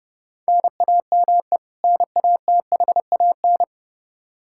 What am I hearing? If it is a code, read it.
Morse code, heard as a keyed tone, about 30 wpm, 700 Hz. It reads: NAME NATHAN
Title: Practice QSO - Exchange 1 - Beginning - 30wpm